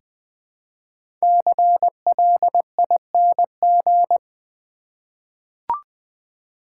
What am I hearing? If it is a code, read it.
Morse code, heard as a keyed tone, about 20 wpm, 700 Hz. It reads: CLING
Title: 5 Letter Words - First 3 Letters Given - 20wpm